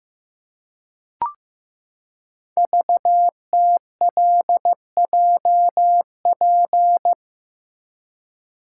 Morse code, heard as a keyed tone, about 15 wpm, 700 Hz. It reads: VTLJP
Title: Code Groups - 5 Letters - 15wpm